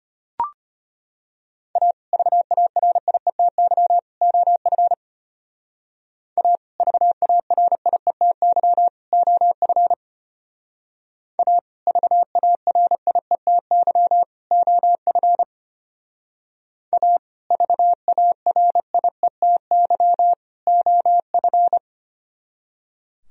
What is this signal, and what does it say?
Morse code, heard as a keyed tone, about 32 wpm, 700 Hz. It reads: A VARIETY OF A VARIETY OF A VARIETY OF A VARIETY OF